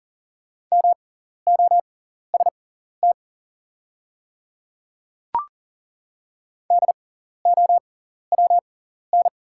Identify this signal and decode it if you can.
Morse code, heard as a keyed tone, about 40 wpm, spaced as for 14 wpm, 690 Hz. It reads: MOST DOWN
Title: 6x Character Spacing - Top 300 Words - 40wpm